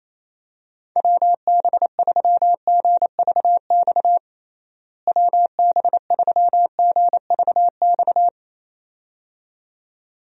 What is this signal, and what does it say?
Morse code, heard as a keyed tone, about 28 wpm, 700 Hz. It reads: WB3GVX WB3GVX
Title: USA Call Signs - 6 Characters - Repeated 2x - 28wpm